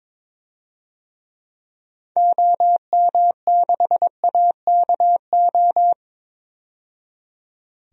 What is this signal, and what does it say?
Morse code, heard as a keyed tone, about 21 wpm, 700 Hz. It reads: OM6AKO
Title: DXCC-100 Call Signs - 6 Characters - 22wpm